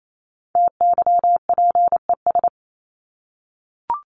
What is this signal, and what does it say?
Morse code, heard as a keyed tone, about 28 wpm, 700 Hz. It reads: TYPES
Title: ES:S Suffix - Top 100 Words with Base Words 28wpm